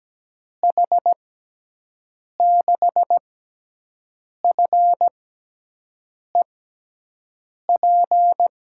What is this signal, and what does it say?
Morse code, heard as a keyed tone, about 17 wpm, spaced as for 6 wpm, 700 Hz. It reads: H6FEP
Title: USA Call Signs - 6x Character Spacing - 17wpm